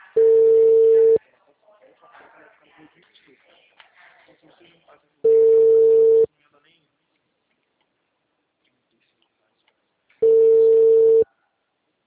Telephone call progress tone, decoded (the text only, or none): ringback tone